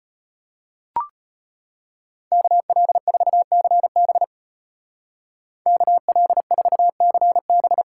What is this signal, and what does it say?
Morse code, heard as a keyed tone, about 32 wpm, 700 Hz. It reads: KL4CB KL4CB